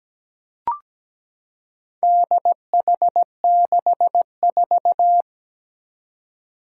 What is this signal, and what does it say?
Morse code, heard as a keyed tone, about 17 wpm, 700 Hz. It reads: DH64
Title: Four Challenging Letter-Number Combinations 17wpm